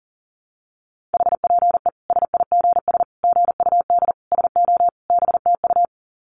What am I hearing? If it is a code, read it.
Morse code, heard as a keyed tone, about 40 wpm, 700 Hz. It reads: HPE SIGS GUD SO BTU
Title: Common QSO Phrases 40wpm